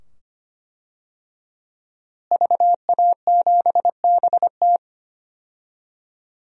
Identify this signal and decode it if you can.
Morse code, heard as a keyed tone, about 25 wpm, 700 Hz. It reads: VA7BT
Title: DXCC-100 Call Signs - 5 Characters - Repeated 2x - 25wpm